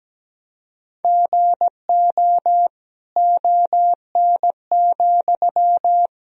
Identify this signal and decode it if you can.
Morse code, heard as a keyed tone, about 17 wpm, 700 Hz. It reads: GO ON,
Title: Sentences from Top 1000 Words - Review Part 4 - 17wpm